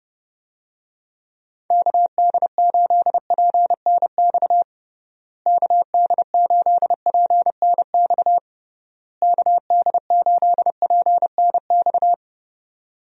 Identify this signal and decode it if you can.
Morse code, heard as a keyed tone, about 30 wpm, 700 Hz. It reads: KD8PNX KD8PNX KD8PNX
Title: USA Call Signs - 6 Characters - Repeated 3x - 30wpm